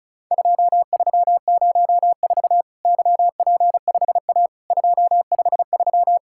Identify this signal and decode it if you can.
Morse code, heard as a keyed tone, about 36 wpm, 700 Hz. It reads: 2304 YP5A 253
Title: Worked All Europe DX Contest - QTC Exchange - 35wpm